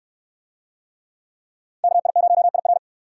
Morse code, heard as a keyed tone, about 61 wpm, 700 Hz. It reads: KI0IR